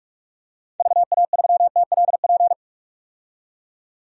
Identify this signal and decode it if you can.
Morse code, heard as a keyed tone, about 47 wpm, 700 Hz. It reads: VA3TLP
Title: DXCC-100 Call Signs - 6 Characters - 45wpm